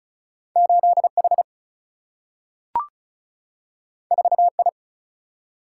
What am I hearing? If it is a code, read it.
Morse code, heard as a keyed tone, about 36 wpm, 700 Hz. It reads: 8 H 4 I
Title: Single-Letter Single-Number 35wpm